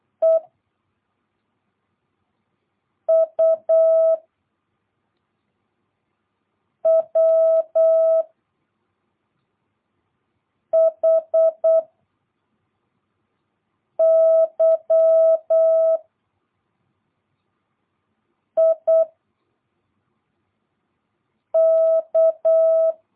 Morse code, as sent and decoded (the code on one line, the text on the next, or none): . / ..- / .-- / .... / -.-- / .. / -.-
E U W H Y I K